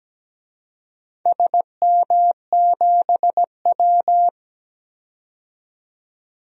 Morse code, heard as a keyed tone, about 17 wpm, 700 Hz. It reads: SM7W
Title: DXCC-100 Call Signs - 4 Characters - 17wpm